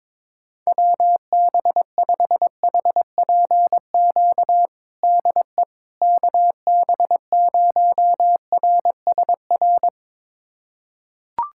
Morse code, heard as a keyed tone, about 22 wpm, 700 Hz. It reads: WB5HPQ DE KB0RSR
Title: USA Callsign DE Callsign - 6 Characters - 22wpm